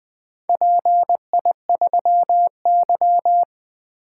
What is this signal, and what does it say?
Morse code, heard as a keyed tone, about 23 wpm, 700 Hz. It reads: PI3Y
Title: Speed-Racing - Four Letter-Number Combinations - 20wpm